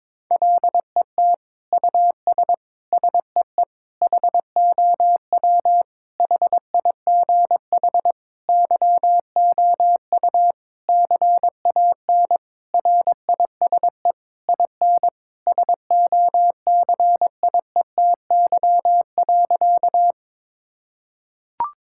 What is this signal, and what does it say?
Morse code, heard as a keyed tone, about 22 wpm, 700 Hz. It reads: LET US SEE HOW HIGH YOU CAN RISE IN SOCIETY.
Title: Sentences from Top 1000 Words - Review Part 6 - 22wpm